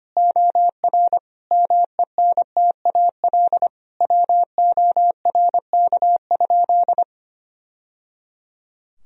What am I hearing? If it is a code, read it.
Morse code, heard as a keyed tone, about 25 wpm, 700 Hz. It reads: OR MENTAL WORK?
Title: Sentences from Top 1000 Words - Review Part 4 - 25wpm